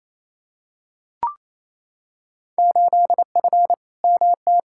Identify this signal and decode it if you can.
Morse code, heard as a keyed tone, about 28 wpm, 700 Hz. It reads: 8F MT
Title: Field Day Exchanges Encore - 28wpm